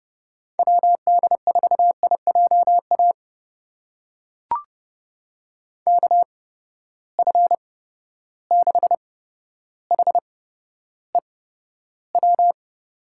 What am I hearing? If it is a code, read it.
Morse code, heard as a keyed tone, about 31 wpm, spaced as for 13 wpm, 700 Hz. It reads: WD4IJA KF6HEW
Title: USA Call Signs - 6 Characters - 8x Character Spacing - 30wpm